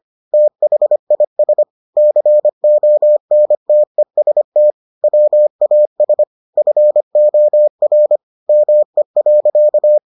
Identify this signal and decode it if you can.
Morse code, heard as a keyed tone, about 25 wpm, 600 Hz. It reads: THIS CONTEST WAS FOR ME.